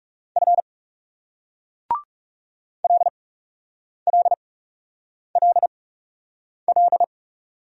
Morse code, heard as a keyed tone, about 38 wpm, spaced as for 8 wpm, 700 Hz. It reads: F LLLL